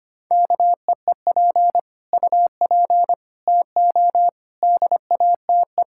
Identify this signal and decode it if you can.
Morse code, heard as a keyed tone, about 25 wpm, 700 Hz. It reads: KEEP UP TO DATE
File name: practice / 25wpm / Sentences with 9-Word Phrases - 25wpm